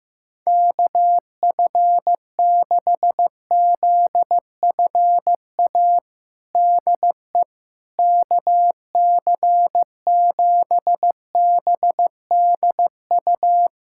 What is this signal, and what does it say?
Morse code, heard as a keyed tone, about 16 wpm, 700 Hz. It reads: KF6ZFA DE KC7BDU
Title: USA Callsign DE Callsign - 6 Characters - 15wpm